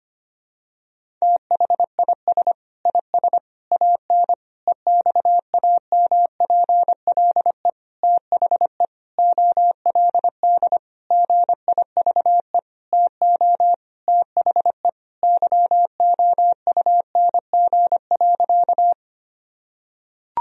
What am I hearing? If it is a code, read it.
Morse code, heard as a keyed tone, about 25 wpm, 700 Hz. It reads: THIS IS AN EXAMPLE THE OLD GIVE TO THE YOUNG.
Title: Sentences from Top 1000 Words - Review Part 1 - 25wpm